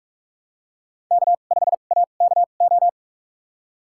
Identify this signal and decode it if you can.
Morse code, heard as a keyed tone, about 45 wpm, spaced as for 32 wpm, 700 Hz. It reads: K5AKO
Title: USA Call Signs - 5 Characters - 2x Character Spacing - 45wpm